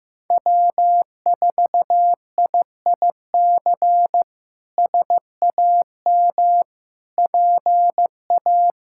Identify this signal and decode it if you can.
Morse code, heard as a keyed tone, about 15 wpm, 700 Hz. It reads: W4IIC SAM PA
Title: K1USN SST QSO Practice - 15wpm